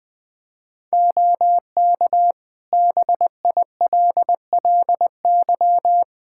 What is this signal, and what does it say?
Morse code, heard as a keyed tone, about 20 wpm, 700 Hz. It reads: OK BILLY
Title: Practice QSO - Exchange 4 - Ending - 20wpm